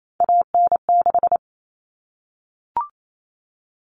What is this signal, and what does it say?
Morse code, heard as a keyed tone, about 28 wpm, 700 Hz. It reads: AN6
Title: All Cut Numbers - 3-Digit 28wpm